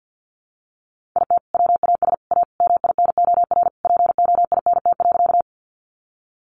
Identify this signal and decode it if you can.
Morse code, heard as a keyed tone, about 50 wpm, 700 Hz. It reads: IT WAS A MINOR POINT.
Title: Sentences from Top 1500 Words 50wpm